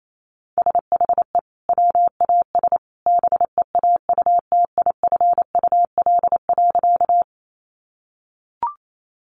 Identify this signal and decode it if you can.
Morse code, heard as a keyed tone, about 28 wpm, 700 Hz. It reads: SHE WAS BEAUTIFUL.
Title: Sentences from Top 900 Words 28wpm